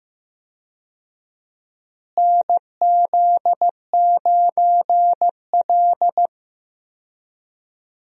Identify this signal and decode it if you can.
Morse code, heard as a keyed tone, about 15 wpm, 700 Hz. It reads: NZ9L
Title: USA Call Signs - 4 Characters - Repeated 3x - 15wpm